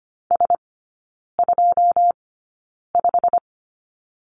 Morse code, heard as a keyed tone, about 25 wpm, spaced as for 9 wpm, 700 Hz. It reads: S25
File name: Lesson 15e - Number 5 - Three Characters 25wpm